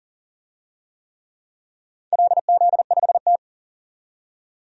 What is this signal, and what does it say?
Morse code, heard as a keyed tone, about 40 wpm, 700 Hz. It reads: LZ5T
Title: DXCC-100 Call Signs - 4 Characters - 40wpm